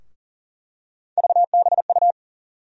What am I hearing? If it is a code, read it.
Morse code, heard as a keyed tone, about 40 wpm, 700 Hz. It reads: VBU
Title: USA Call Sign Postfixes - 3 Letters 40wpm